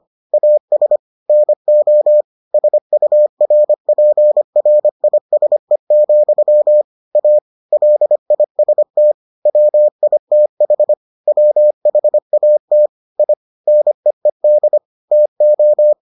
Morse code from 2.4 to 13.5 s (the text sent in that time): SURPRISE, A LIST WITH WHAT I